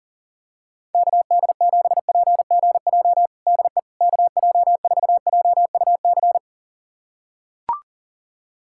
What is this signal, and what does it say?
Morse code, heard as a keyed tone, about 40 wpm, 700 Hz. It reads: KD7PGJ DE KJ4JUC
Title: USA Callsign DE Callsign 40wpm